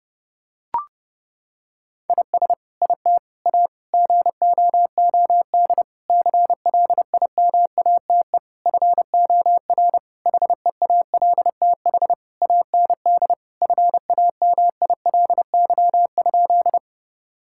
Morse code, heard as a keyed tone, about 30 wpm, 700 Hz. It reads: IS IT A GOOD CLIMATE FOR HEALTH AND FAMILY?
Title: Sentences from Top 1600 Words 30wpm